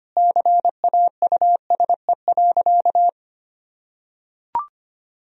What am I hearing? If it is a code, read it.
Morse code, heard as a keyed tone, about 25 wpm, 700 Hz. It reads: CAUSE.
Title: Sentences from Top 1000 Words - Review Part 2 - 25wpm